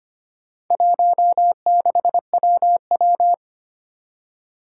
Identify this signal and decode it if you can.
Morse code, heard as a keyed tone, about 25 wpm, 700 Hz. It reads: 16WW
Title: Four Challenging Letter-Number Combinations 25wpm